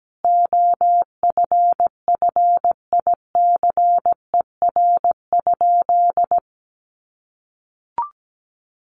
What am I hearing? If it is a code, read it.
Morse code, heard as a keyed tone, about 17 wpm, 700 Hz. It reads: OFFICER?
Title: Sentences from Top 1200 Words 17wpm